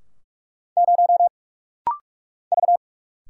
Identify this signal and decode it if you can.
Morse code, heard as a keyed tone, about 47 wpm, 700 Hz. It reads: 0 V
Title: Single Letter-Number - Rapid-Fire 45wpm